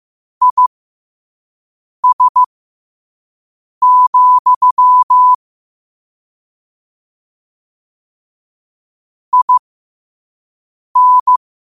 Morse code, heard as a keyed tone, about 15 wpm, spaced as for 5 wpm, 1000 Hz. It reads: IS, IN
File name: qotdfiles / 5wpm